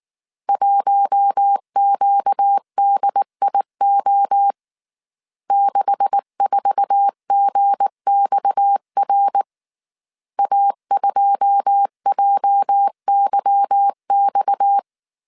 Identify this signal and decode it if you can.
Morse code, heard as a keyed tone, about 19 wpm, 800 Hz. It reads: 1QDIO 64GXR A2JYX
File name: CW / DET / 1 9 1500C S A 18wpm 2459